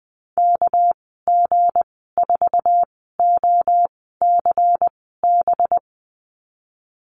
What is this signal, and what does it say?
Morse code, heard as a keyed tone, about 20 wpm, spaced as for 14 wpm, 700 Hz. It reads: KG4OCB
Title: 207 Lesson 33g - Letter j - Call Signs 20wpm